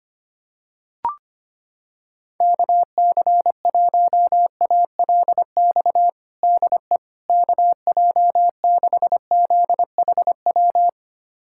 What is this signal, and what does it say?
Morse code, heard as a keyed tone, about 25 wpm, 700 Hz. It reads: KC1ALX DE KJ6ZHW
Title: USA Callsign DE Callsign - 6 Characters - 25wpm